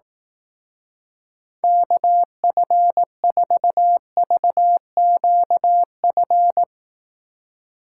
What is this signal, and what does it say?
Morse code, heard as a keyed tone, about 18 wpm, 700 Hz. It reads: KF4VQF
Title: Speed Racing - USA 2x3 Call Signs 15wpm